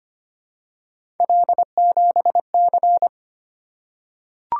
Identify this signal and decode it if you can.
Morse code, heard as a keyed tone, about 25 wpm, 700 Hz. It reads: L7C
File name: Three Letter-Number Combinations - 25wpm